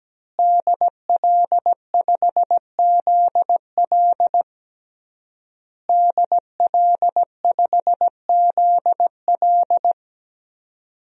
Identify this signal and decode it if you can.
Morse code, heard as a keyed tone, about 17 wpm, 700 Hz. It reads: DL5ZL DL5ZL
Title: DXCC-100 Call Signs - 5 Characters - Repeated 3x - 17wpm